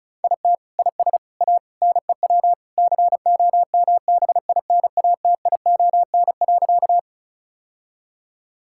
Morse code, heard as a keyed tone, about 36 wpm, 700 Hz. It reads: IT IS A NEW COMBINATION.
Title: Sentences from Top 1000 Words - Review Part 6 - 35wpm